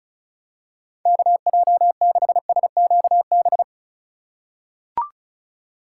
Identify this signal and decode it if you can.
Morse code, heard as a keyed tone, about 34 wpm, 700 Hz. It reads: KJ6SQB
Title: USA Call Signs - 6 Characters - 2x Character Spacing - 35wpm